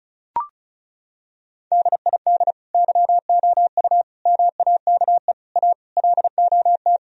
Morse code, heard as a keyed tone, about 36 wpm, 700 Hz. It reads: DID YOU MAKE A LOT
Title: Sentences from Top 1000 Words - Review Part 4 - 35wpm